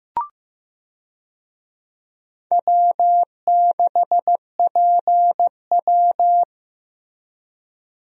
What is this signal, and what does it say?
Morse code, heard as a keyed tone, about 15 wpm, 700 Hz. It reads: W6PW
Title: USA Call Signs - 4 Characters - Repeated 2x - 15wpm